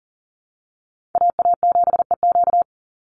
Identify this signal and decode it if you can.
Morse code, heard as a keyed tone, about 40 wpm, 700 Hz. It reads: AA7EQ
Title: NA SOTA Chaser Callsigns 40wpm